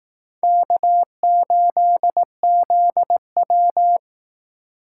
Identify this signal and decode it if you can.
Morse code, heard as a keyed tone, about 18 wpm, 700 Hz. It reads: K8ZW
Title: Speed Racing - USA 1x2 Call Signs 15wpm